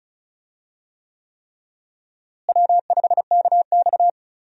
Morse code, heard as a keyed tone, about 34 wpm, 700 Hz. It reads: W5KX
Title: USA Call Signs - 4 Characters - Repeated 3x - 35wpm